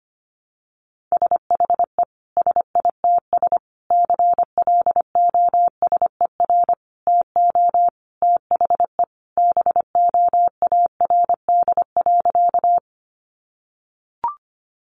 Morse code, heard as a keyed tone, about 25 wpm, 700 Hz. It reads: SHE SITS CLOSER TO THE BOARD.
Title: Additional Easy Phrases 25wpm